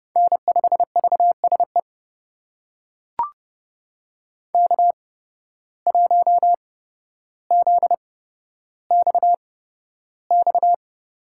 Morse code, heard as a keyed tone, about 30 wpm, spaced as for 13 wpm, 700 Hz. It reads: N5VSE K1ZXX
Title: USA Call Signs - 5 Characters - 8x Character Spacing - 30wpm